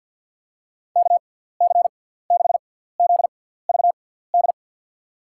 Morse code, heard as a keyed tone, about 49 wpm, spaced as for 17 wpm, 700 Hz. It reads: KC6ZVD
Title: USA Call Signs - 6 Characters - 6x Character Spacing - 50wpm